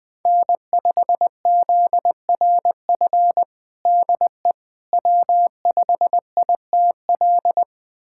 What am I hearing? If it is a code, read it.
Morse code, heard as a keyed tone, about 21 wpm, 700 Hz. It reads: N5ZRF DE W5ITL
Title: USA Callsign DE Callsign - 5 Characters - 20wpm